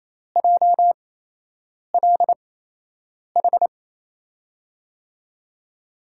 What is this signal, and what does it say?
Morse code, heard as a keyed tone, about 28 wpm, spaced as for 8 wpm, 700 Hz. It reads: JLH
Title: USA Call Signs - 5 Characters - 8x Character Spacing - 28wpm